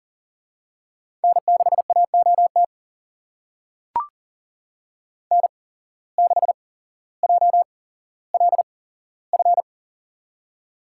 Morse code, heard as a keyed tone, about 40 wpm, spaced as for 17 wpm, 700 Hz. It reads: N6AOT N6JLF
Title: USA Call Signs - 5 Characters - 8x Character Spacing - 40wpm